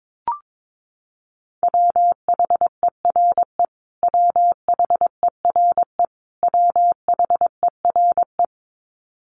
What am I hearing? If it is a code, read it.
Morse code, heard as a keyed tone, about 22 wpm, 700 Hz. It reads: WHERE WHERE WHERE
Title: Top 500 Words Repeated 22wpm